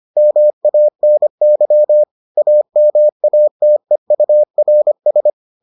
Morse code, heard as a keyed tone, about 25 wpm, 600 Hz. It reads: MANY AMATEURS